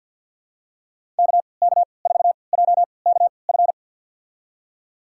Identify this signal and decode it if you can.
Morse code, heard as a keyed tone, about 50 wpm, spaced as for 28 wpm, 700 Hz. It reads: KK4JKF
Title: USA Call Signs - 3x Character Spacing - 50wpm